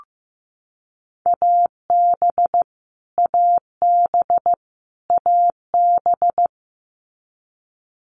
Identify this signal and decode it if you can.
Morse code, heard as a keyed tone, about 15 wpm, 700 Hz. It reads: AB AB AB
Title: Canadian Provinces and Territories - Repeated 3x - 15wpm